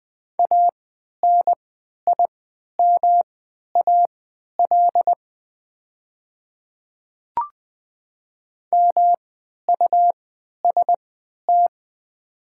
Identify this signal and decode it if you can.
Morse code, heard as a keyed tone, about 20 wpm, spaced as for 11 wpm, 700 Hz. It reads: ANIMAL MUST
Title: 3x Character Spacing - Top 200 Words - 20wpm